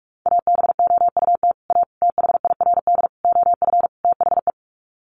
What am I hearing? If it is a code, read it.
Morse code, heard as a keyed tone, about 44 wpm, 700 Hz. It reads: ABOUT A THIRD OF THE
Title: Sets of 5 Words Encore2 45wpm